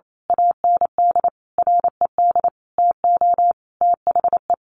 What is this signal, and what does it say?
Morse code, heard as a keyed tone, about 28 wpm, 700 Hz. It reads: AND RED TO THE
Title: A Sound Of Thunder 28wpm